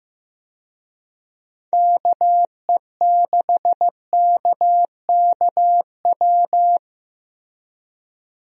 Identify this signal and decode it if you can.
Morse code, heard as a keyed tone, about 15 wpm, 700 Hz. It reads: KE6KKW
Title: USA Call Signs - 6 Characters - Repeated 3x - 15wpm